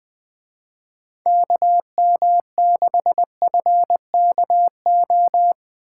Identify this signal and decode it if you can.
Morse code, heard as a keyed tone, about 22 wpm, 700 Hz. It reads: KM6FKO
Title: Speed Racing - USA Call Signs 20wpm